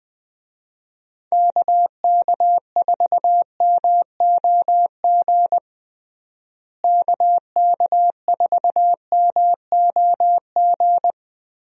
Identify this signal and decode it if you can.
Morse code, heard as a keyed tone, about 21 wpm, 700 Hz. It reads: KK4MOG KK4MOG
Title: USA Call Signs - 6 Characters - Repeated 3x - 20wpm